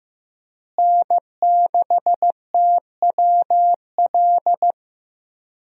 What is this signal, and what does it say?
Morse code, heard as a keyed tone, about 15 wpm, 700 Hz. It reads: N6TWL
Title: USA Call Signs - 5 Characters - Repeated 3x - 15wpm